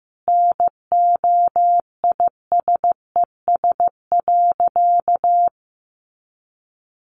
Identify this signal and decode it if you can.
Morse code, heard as a keyed tone, about 15 wpm, 700 Hz. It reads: NOISES.
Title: Easy Phrases 15wpm